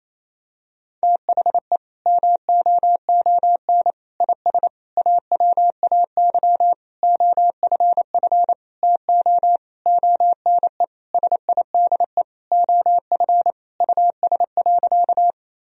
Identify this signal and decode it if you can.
Morse code, heard as a keyed tone, about 28 wpm, 700 Hz. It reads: THE MOON IS AWAY OFF TO ONE SIDE OF US.
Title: Sentences from Top 1000 Words - Review Part 3 - 28wpm